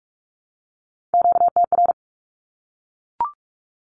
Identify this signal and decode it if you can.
Morse code, heard as a keyed tone, about 44 wpm, 700 Hz. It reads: QTR
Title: Q-Codes Repeated 3x 45wpm